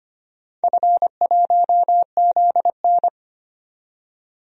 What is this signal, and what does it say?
Morse code, heard as a keyed tone, about 25 wpm, 700 Hz. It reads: F1ZN
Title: DXCC-100 Call Signs - 4 Characters - 25wpm